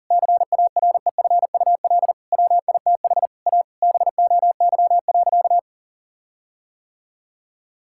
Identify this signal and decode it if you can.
Morse code, heard as a keyed tone, about 37 wpm, 700 Hz. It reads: CAREFUL WITH A BOY.
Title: Sentences from Top 1600 Words 40wpm